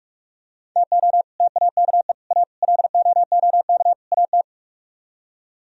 Morse code, heard as a keyed tone, about 45 wpm, 700 Hz. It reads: TO TAKE A LOOK AT